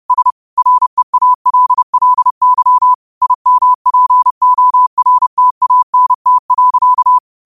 Morse code, heard as a keyed tone, about 30 wpm, 1000 Hz. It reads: S REALLY IMPORTANT.
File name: qotdfiles / 30wpm